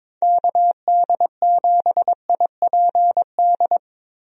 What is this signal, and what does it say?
Morse code, heard as a keyed tone, about 22 wpm, 700 Hz. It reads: KD7IPD
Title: USA Call Signs - 6 Characters - Repeated 2x - 22wpm